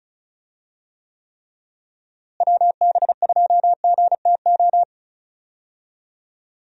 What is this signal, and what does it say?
Morse code, heard as a keyed tone, about 36 wpm, 700 Hz. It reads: WB2GTO
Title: USA Call Signs - 6 Characters - Repeated 3x - 35wpm